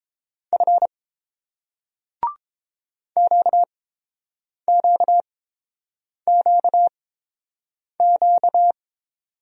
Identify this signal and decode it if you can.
Morse code, heard as a keyed tone, about 30 wpm, spaced as for 7 wpm, 700 Hz. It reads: F QQQQ